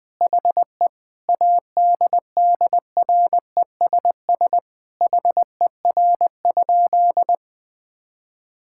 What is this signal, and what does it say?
Morse code, heard as a keyed tone, about 20 wpm, 700 Hz. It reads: HE ADDRESS HER?
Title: Sentences from Top 1000 Words - Review Part 6 - 20wpm